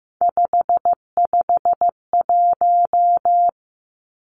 Morse code, heard as a keyed tone, about 15 wpm, 700 Hz. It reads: 551
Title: Three Challenging Letter-Number Combinations 15wpm